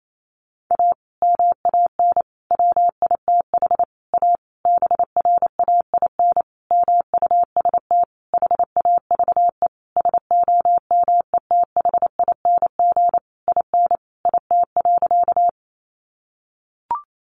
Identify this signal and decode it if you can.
Morse code, heard as a keyed tone, about 28 wpm, 700 Hz. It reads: A MAN WITH A BRAIN MUST HAVE SOMETHING IN IT.
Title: Sentences from Top 1100 Words 28wpm